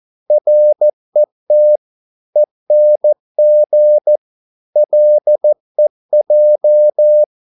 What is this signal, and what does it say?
Morse code, heard as a keyed tone, about 14 wpm, 600 Hz. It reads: RET ENG LEJ
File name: Korte ord DK 14wpm